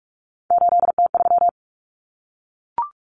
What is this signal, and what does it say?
Morse code, heard as a keyed tone, about 44 wpm, 700 Hz. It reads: 8T3
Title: Common Cut Numbers - 3-Digit 45wpm